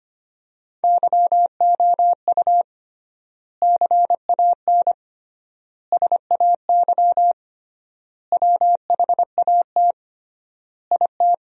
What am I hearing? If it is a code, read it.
Morse code, heard as a keyed tone, about 25 wpm, 700 Hz. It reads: YOU CAN SAY WHAT IT
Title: 3x Word-Spacing - Sentences from Top 200 Words - 25wpm